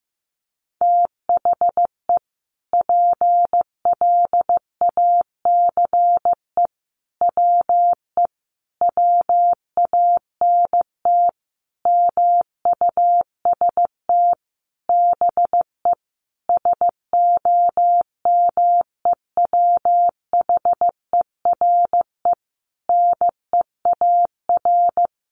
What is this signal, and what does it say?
Morse code, heard as a keyed tone, about 15 wpm, 700 Hz. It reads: THE PLACE WE WANT MUST BE SOMEWHERE NEAR
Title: Sentences from Top 1400 Words 15wpm